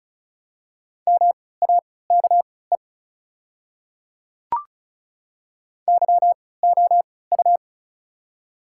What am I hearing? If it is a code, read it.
Morse code, heard as a keyed tone, about 36 wpm, spaced as for 20 wpm, 690 Hz. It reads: MAKE YOU
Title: 3x Character Spacing - Top 100 Words - 35wpm